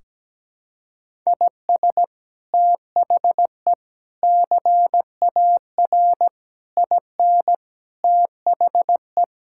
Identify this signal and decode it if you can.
Morse code, heard as a keyed tone, about 17 wpm, 700 Hz. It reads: IS THE CAR IN THE
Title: Sentences from Top 1600 Words 17wpm